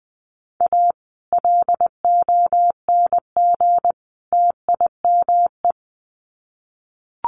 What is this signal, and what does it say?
Morse code, heard as a keyed tone, about 20 wpm, 700 Hz. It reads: A LONG TIME
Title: Sets of 3 Words 20wpm